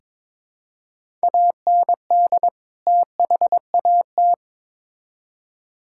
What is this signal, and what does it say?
Morse code, heard as a keyed tone, about 22 wpm, 700 Hz. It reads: AND THAT
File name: Build A Phrase - 2-Word Phrases - 22wpm